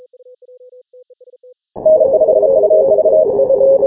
Morse code, heard as a keyed tone, about 33 wpm, 630 Hz. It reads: N5UV
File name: final / W1UJ-168-140-MorseRunner